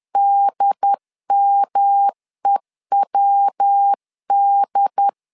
Morse code, heard as a keyed tone, about 10 wpm, 800 Hz. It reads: DMEWD
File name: LEZIONI / 3 MWRO / 3 MW 10wpm 1959